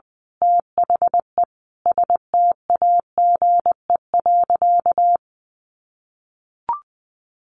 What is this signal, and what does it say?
Morse code, heard as a keyed tone, about 20 wpm, 700 Hz. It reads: THE STAGE.
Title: Sentences from Top 1200 Words 20wpm